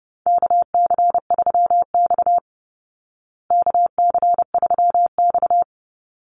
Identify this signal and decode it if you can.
Morse code, heard as a keyed tone, about 30 wpm, 700 Hz. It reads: KC3X KC3X
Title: Call Signs Repeated 3x - 30wpm